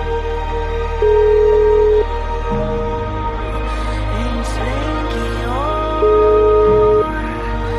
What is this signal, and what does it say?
Ringback tone (Europe, 425 Hz), with noise.